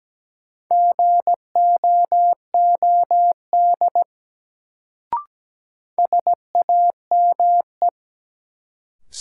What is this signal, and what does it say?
Morse code, heard as a keyed tone, about 17 wpm, 700 Hz. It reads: GOOD SAME
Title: Sentences from Top 200 Words - New Word Drill - 17wpm